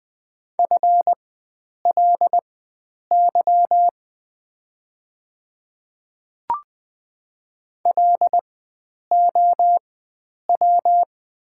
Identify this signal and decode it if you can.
Morse code, heard as a keyed tone, about 20 wpm, spaced as for 9 wpm, 690 Hz. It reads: FLY LOW
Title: 4x Character Spacing - Top 500 Words - 20wpm